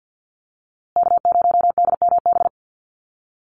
Morse code, heard as a keyed tone, about 50 wpm, 700 Hz. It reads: K0DMB
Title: USA Call Signs Repeated 2x 50wpm